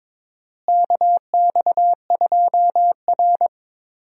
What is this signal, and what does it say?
Morse code, heard as a keyed tone, about 22 wpm, 700 Hz. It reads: KX2R
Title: USA Call Signs - 4 Characters - Repeated 3x - 22wpm